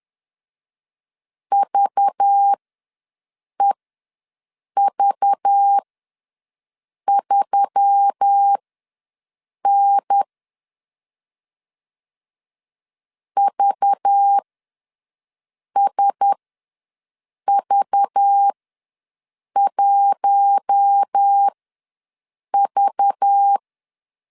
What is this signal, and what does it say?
Morse code, heard as a keyed tone, about 10 wpm, spaced as for 6 wpm, 800 Hz. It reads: VEV3N VSV1V
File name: LEZIONI / 5 LBVC / 5 LBV ws 10wpm 3645